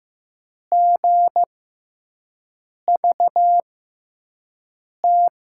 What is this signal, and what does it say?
Morse code, heard as a keyed tone, about 15 wpm, spaced as for 5 wpm, 700 Hz. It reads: GVT
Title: USA Call Signs - 6x Character Spacing - 15wpm